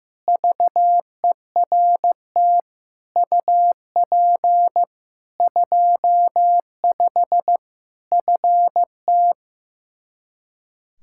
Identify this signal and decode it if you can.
Morse code, heard as a keyed tone, about 15 wpm, 700 Hz. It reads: VERT UP 25 FT
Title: K1USN SST QSO Practice 15wpm